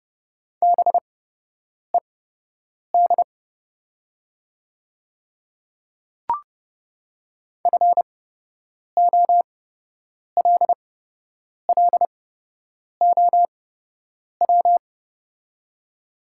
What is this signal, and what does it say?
Morse code, heard as a keyed tone, about 30 wpm, spaced as for 8 wpm, 700 Hz. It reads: BED FOLLOW
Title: 8x Character Spacing - Top 500 Words - 30wpm